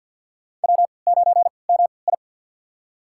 Morse code, heard as a keyed tone, about 52 wpm, spaced as for 29 wpm, 700 Hz. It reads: W9MI
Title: USA Call Signs - 4 Characters - 3x Character Spacing - 50wpm